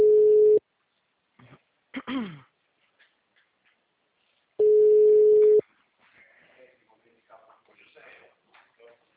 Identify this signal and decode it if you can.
Ringback tone (Europe, 425 Hz), with speech.
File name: out- -2008- - - .12375